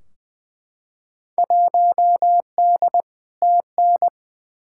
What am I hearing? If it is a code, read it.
Morse code, heard as a keyed tone, about 20 wpm, 700 Hz. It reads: 1D TN
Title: Field Day Exchanges - 20wpm